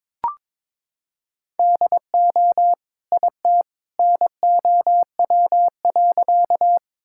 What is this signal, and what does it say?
Morse code, heard as a keyed tone, about 22 wpm, 700 Hz. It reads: DO IT NOW.
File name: practice / 22wpm / Sentences with 2-Word Phrases - 22wpm